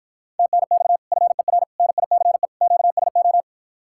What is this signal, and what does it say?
Morse code, heard as a keyed tone, about 54 wpm, 700 Hz. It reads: TNX FER NICE QSO